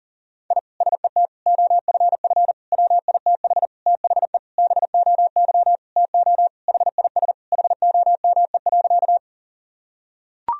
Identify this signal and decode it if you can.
Morse code, heard as a keyed tone, about 37 wpm, 700 Hz. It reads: I SET OFF WITH THE BOY TO HIS HOME.
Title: Sentences from Top 1000 Words - Review Part 1 - 40wpm